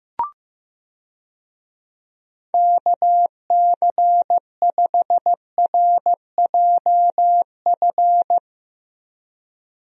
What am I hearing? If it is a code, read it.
Morse code, heard as a keyed tone, about 15 wpm, 700 Hz. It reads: KC5RJF
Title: USA Call Signs - 6 Characters - Repeated 2x - 15wpm